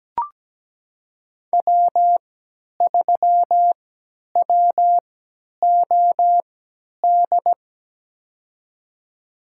Morse code, heard as a keyed tone, about 17 wpm, spaced as for 10 wpm, 700 Hz. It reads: W3WOD
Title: USA Call Signs - 5 Characters - 3x Character Spacing - 17wpm